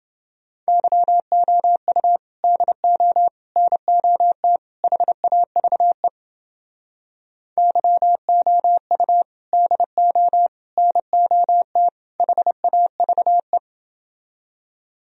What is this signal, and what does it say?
Morse code, heard as a keyed tone, about 29 wpm, 700 Hz. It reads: YOU DO NOT HAVE YOU DO NOT HAVE